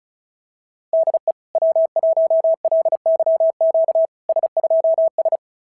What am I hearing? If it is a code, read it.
Morse code, heard as a keyed tone, about 38 wpm, 650 Hz. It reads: DE W1LYQ S2S
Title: SOTA QSO Practice 35wpm